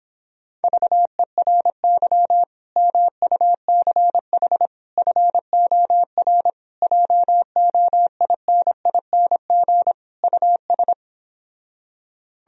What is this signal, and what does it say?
Morse code, heard as a keyed tone, about 26 wpm, 700 Hz. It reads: VERY MUCH FOR JOINING US
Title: Speed-Racing - Sets of 5 Words 22wpm